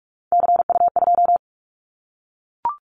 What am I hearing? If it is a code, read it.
Morse code, heard as a keyed tone, about 44 wpm, 700 Hz. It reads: CU2
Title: Callsign Prefixes for Top 100 DXCC - 45wpm